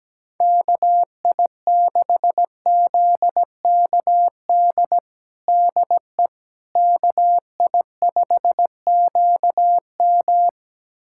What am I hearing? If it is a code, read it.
Morse code, heard as a keyed tone, about 17 wpm, 700 Hz. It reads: KI6ZKD DE KI5QM
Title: USA Callsign DE Callsign 17wpm